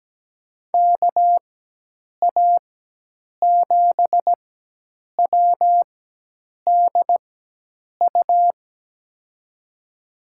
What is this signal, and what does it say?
Morse code, heard as a keyed tone, about 17 wpm, spaced as for 8 wpm, 700 Hz. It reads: KA7WDU
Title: USA Call Signs - 6 Characters - 4x Character Spacing - 17wpm